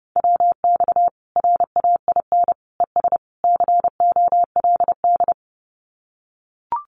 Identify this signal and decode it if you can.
Morse code, heard as a keyed tone, about 30 wpm, 700 Hz. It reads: WX RAIN ES COLD
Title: Common QSO Phrases 30wpm